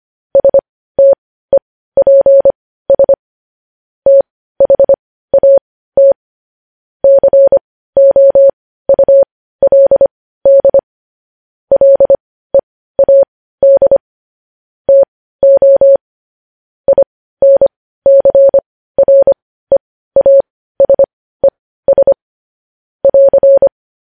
Morse code, heard as a keyed tone, about 25 wpm, spaced as for 15 wpm, 550 Hz. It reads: STEPS THAT COULD LEAD TO INCREASES <AR>
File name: podcast / News headlines-1x-25x15